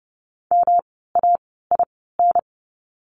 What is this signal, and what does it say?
Morse code, heard as a keyed tone, about 30 wpm, spaced as for 17 wpm, 700 Hz. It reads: MAIN
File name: course / 30:10wpm / 108 Lesson 18f - Letter m - Words 30wpm